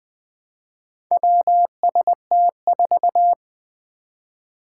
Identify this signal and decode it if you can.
Morse code, heard as a keyed tone, about 20 wpm, 700 Hz. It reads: WST4
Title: Four Challenging Letter-Number Combinations 20wpm